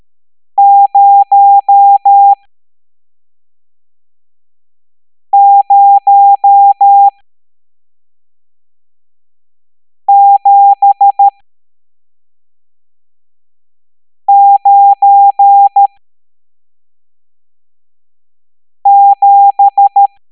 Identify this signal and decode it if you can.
Morse code, heard as a keyed tone, about 13 wpm, spaced as for 3 wpm, 800 Hz. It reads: 00797
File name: hamdata / L11-EX01-67890